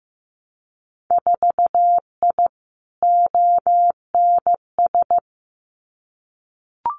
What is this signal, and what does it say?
Morse code, heard as a keyed tone, about 15 wpm, 700 Hz. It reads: 4I ONS
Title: Winter Field Day Exchanges Encore 15wpm